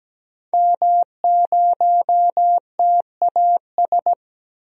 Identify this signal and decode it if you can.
Morse code, heard as a keyed tone, about 17 wpm, 700 Hz. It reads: M0TAS
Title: DXCC-100 Call Signs - 5 Characters - 17wpm